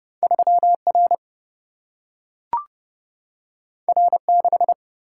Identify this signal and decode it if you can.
Morse code, heard as a keyed tone, about 30 wpm, 700 Hz. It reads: 3 R R 6
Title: Single-Letter Single-Number 30wpm